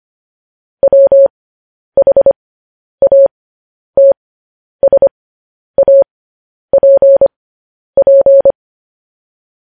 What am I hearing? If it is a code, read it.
Morse code, heard as a keyed tone, about 25 wpm, spaced as for 10 wpm, 550 Hz. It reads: WHATSAPP
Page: news headlines in morse code at 10 wpm